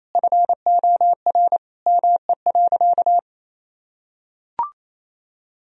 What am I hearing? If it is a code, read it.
Morse code, heard as a keyed tone, about 28 wpm, 700 Hz. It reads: FOR ME.
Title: Sentences from Top 1000 Words - Review Part 4 - 28wpm